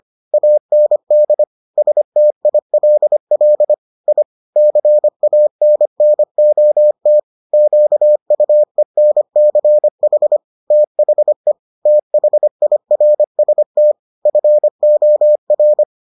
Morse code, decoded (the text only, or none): AND STILL I CANNOT QUENCH THE THIRST FOR